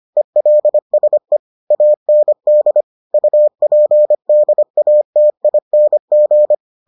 Morse code, heard as a keyed tone, about 25 wpm, 600 Hz. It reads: ELSE AND UPDATING